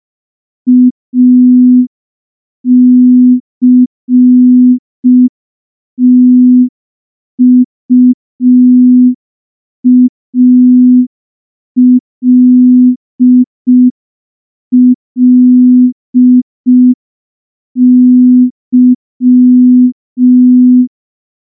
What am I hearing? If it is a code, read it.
Morse code, heard as a keyed tone, about 5 wpm, 250 Hz. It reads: ACTUALLY
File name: morse / morse-training-05WPM